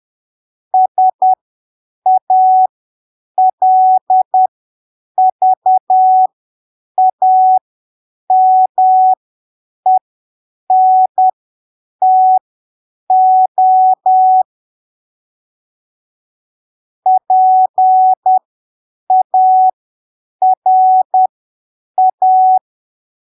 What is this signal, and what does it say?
Morse code, heard as a keyed tone, about 10 wpm, spaced as for 7 wpm, 750 Hz. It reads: SALVAMENTO PARA
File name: arquivos / texto / texto 03 10ppm